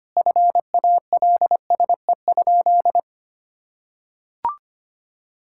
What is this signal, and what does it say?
Morse code, heard as a keyed tone, about 25 wpm, 700 Hz. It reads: FALSE?
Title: Sentences from Top 1600 Words 25wpm